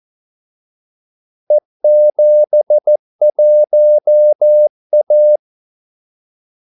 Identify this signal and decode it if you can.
Morse code, heard as a keyed tone, about 14 wpm, 600 Hz. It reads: E71A
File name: Callsigns-14wpm